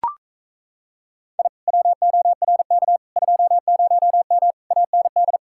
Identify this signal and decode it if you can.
Morse code, heard as a keyed tone, about 37 wpm, 700 Hz. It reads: I WORK 20M AND